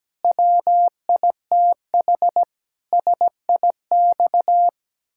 Morse code, heard as a keyed tone, about 17 wpm, 700 Hz. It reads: WITH SIX
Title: Pangram Challenge v2 - 17wpm